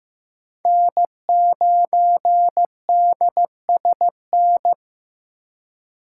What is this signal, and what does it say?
Morse code, heard as a keyed tone, about 15 wpm, 700 Hz. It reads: N9DSN
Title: USA Call Signs - 3x Character Spacing - 15wpm